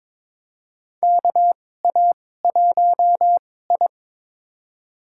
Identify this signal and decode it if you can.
Morse code, heard as a keyed tone, about 22 wpm, spaced as for 16 wpm, 700 Hz. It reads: KA1I
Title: USA Call Signs - 4 Characters - 2x Character Spacing - 22wpm